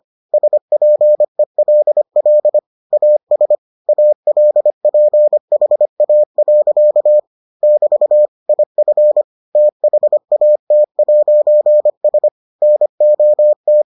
Morse code, heard as a keyed tone, about 25 wpm, 600 Hz. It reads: SPELL AS ALPHA. <BT> IF THAT'S NOT